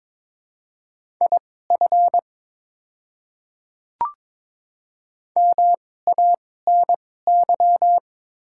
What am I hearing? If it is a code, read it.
Morse code, heard as a keyed tone, about 22 wpm, spaced as for 16 wpm, 690 Hz. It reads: IF MANY
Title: 2x Character Spacing - Top 500 Words - 22wpm